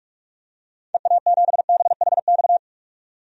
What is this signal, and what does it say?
Morse code, heard as a keyed tone, about 46 wpm, 700 Hz. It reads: EA7BHX
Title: DXCC-100 Call Signs - 6 Characters - 45wpm